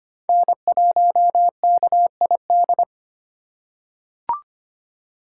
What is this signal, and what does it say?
Morse code, heard as a keyed tone, about 25 wpm, 700 Hz. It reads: N1KID
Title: USA Call Signs - 3x Character Spacing - 25wpm